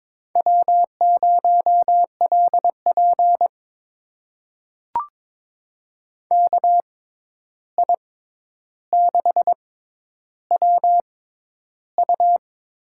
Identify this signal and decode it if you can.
Morse code, heard as a keyed tone, about 22 wpm, spaced as for 11 wpm, 700 Hz. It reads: W0LP KI6WU
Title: USA Call Signs - 6x Character Spacing - 22wpm